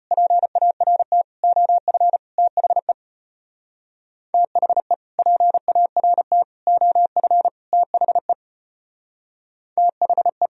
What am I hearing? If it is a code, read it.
Morse code, heard as a keyed tone, about 31 wpm, 700 Hz. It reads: PART OF THE THE PART OF THE THE
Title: Speed-Racing - Sets of 4 Words 25wpm